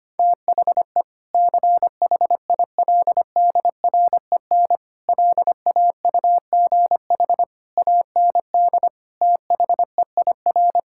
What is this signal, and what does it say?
Morse code, heard as a keyed tone, about 25 wpm, 700 Hz. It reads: THE CHILDREN LAUGH AND THEIR
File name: practice / 25wpm / Sentences with 2-Word Phrases - 25wpm